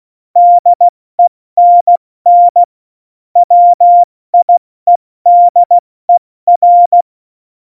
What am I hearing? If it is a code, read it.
Morse code, heard as a keyed tone, about 16 wpm, 700 Hz. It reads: DENN WIEDER
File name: Morsekurs / Witze / Witz 02. Januar 2018, 70 F80